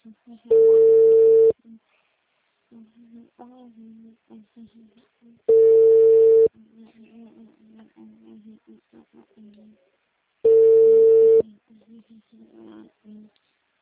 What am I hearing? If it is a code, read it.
Ringback tone (China, 450 Hz), with speech.